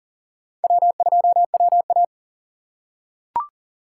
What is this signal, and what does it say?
Morse code, heard as a keyed tone, about 37 wpm, 700 Hz. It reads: W2WA
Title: USA Call Signs - 4 Characters - 4x Character Spacing - 40wpm